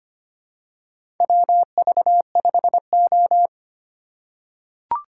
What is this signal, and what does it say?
Morse code, heard as a keyed tone, about 25 wpm, 700 Hz. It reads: WV5O
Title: USA Call Signs - 4 Characters - 4x Character Spacing - 25wpm